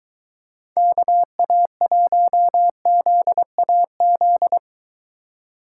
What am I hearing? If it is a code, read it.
Morse code, heard as a keyed tone, about 23 wpm, 700 Hz. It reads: KA1ZAZ